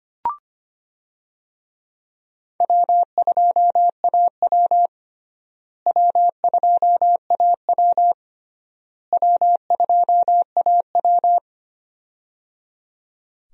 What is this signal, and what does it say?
Morse code, heard as a keyed tone, about 25 wpm, 700 Hz. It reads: W2AW W2AW W2AW
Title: USA Call Signs - 4 Characters - Repeated 3x - 25wpm